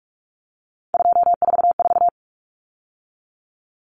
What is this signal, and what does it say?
Morse code, heard as a keyed tone, about 44 wpm, 700 Hz. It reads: 244
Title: Numbers - 3-Digit 45wpm